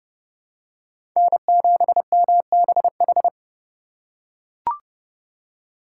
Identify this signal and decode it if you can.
Morse code, heard as a keyed tone, about 30 wpm, 700 Hz. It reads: N7MBH
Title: USA Call Signs - 5 Characters - 4x Character Spacing - 30wpm